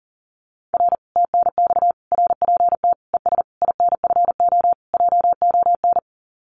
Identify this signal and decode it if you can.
Morse code, heard as a keyed tone, about 40 wpm, 700 Hz. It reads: R TNX RPT ES INFO JON
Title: Common QSO Phrases Encore 40wpm